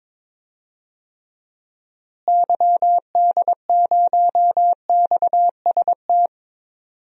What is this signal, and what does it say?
Morse code, heard as a keyed tone, about 21 wpm, 700 Hz. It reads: YD0XST
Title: DXCC-100 Call Signs - 6 Characters - Repeated 3x - 22wpm